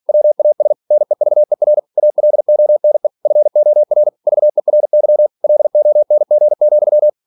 Morse code, heard as a keyed tone, about 46 wpm, 600 Hz. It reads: WAS NEVER ALONE FOR VERY LONG,
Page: Never 47wpm